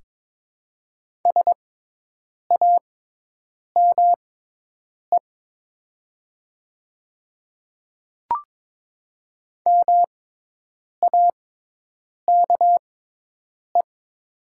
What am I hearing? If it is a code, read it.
Morse code, heard as a keyed tone, about 22 wpm, spaced as for 8 wpm, 700 Hz. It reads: SAME MAKE
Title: 6x Character Spacing - Top 200 Words - 22wpm